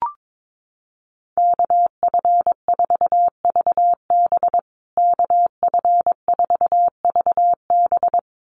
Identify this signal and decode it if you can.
Morse code, heard as a keyed tone, about 22 wpm, 700 Hz. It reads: KF4VB KF4VB
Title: USA Call Signs Repeated 2x 22wpm